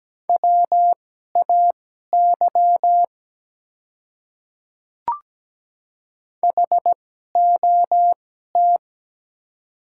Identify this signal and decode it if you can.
Morse code, heard as a keyed tone, about 17 wpm, spaced as for 12 wpm, 690 Hz. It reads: WAY HOT
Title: 2x Character Spacing - Top 100 Words - 17wpm